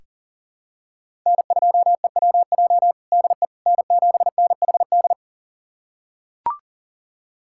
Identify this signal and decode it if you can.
Morse code, heard as a keyed tone, about 40 wpm, 700 Hz. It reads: N2EWJ DE N7NHD
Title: USA Callsign DE Callsign - 5 Characters - 40wpm